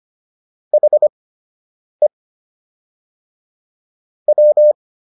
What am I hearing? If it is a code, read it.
Morse code, heard as a keyed tone, about 25 wpm, spaced as for 8 wpm, 600 Hz. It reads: HE W